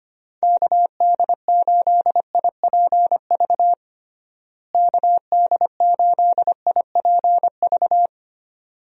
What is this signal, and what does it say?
Morse code, heard as a keyed tone, about 25 wpm, 700 Hz. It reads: KD8IPV KD8IPV
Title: USA Call Signs - 6 Characters - Repeated 2x - 25wpm